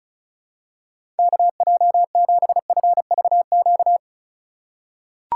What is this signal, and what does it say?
Morse code, heard as a keyed tone, about 36 wpm, 700 Hz. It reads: KJ7FVQ
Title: USA Call Signs - 6 Characters - 3x Character Spacing - 35wpm